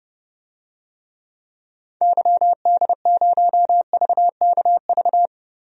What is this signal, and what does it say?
Morse code, heard as a keyed tone, about 30 wpm, 700 Hz. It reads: YD0VKV
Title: DXCC-100 Call Signs - 6 Characters - 30wpm